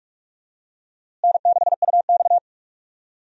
Morse code, heard as a keyed tone, about 41 wpm, 700 Hz. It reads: N6UX
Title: USA Call Signs - 4 Characters - Repeated 2x - 45wpm